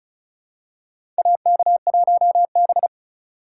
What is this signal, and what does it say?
Morse code, heard as a keyed tone, about 36 wpm, 700 Hz. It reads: AK1B
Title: USA Call Signs - 4 Characters - 3x Character Spacing - 35wpm